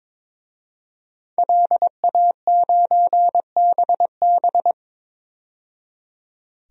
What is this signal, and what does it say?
Morse code, heard as a keyed tone, about 22 wpm, 700 Hz. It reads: LA9BB
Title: DXCC-100 Call Signs - 5 Characters - 22wpm